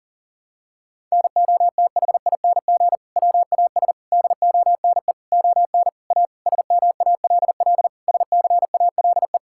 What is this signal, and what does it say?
Morse code, heard as a keyed tone, about 40 wpm, 700 Hz. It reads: NOTHING WAS DONE ON A SMALL SCALE
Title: Sentences from Top 1000 Words - Review Part 5 - 40wpm